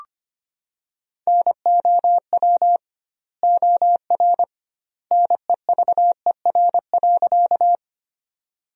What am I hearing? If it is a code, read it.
Morse code, heard as a keyed tone, about 25 wpm, 700 Hz. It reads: NOW OR NEVER.
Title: 2x Word-Spacing - Sentences from Top 300 Words - 25wpm